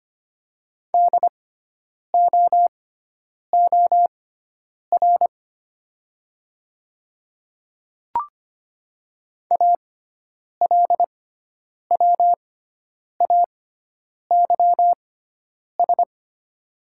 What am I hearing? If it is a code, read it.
Morse code, heard as a keyed tone, about 25 wpm, spaced as for 9 wpm, 700 Hz. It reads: DOOR ALWAYS
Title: 6x Character Spacing - Top 500 Words - 25wpm